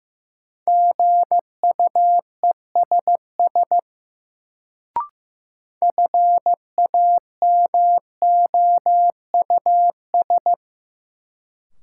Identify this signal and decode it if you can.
Morse code, heard as a keyed tone, about 15 wpm, 700 Hz. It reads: GUESS FAMOUS
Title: Sentences from Top 1500 Words - New Word Drill Alt - 15wpm